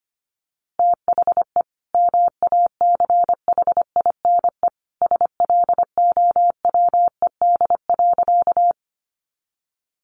Morse code, heard as a keyed tone, about 25 wpm, 700 Hz. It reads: THE MACHINE SLOWED.
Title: A Sound Of Thunder 25wpm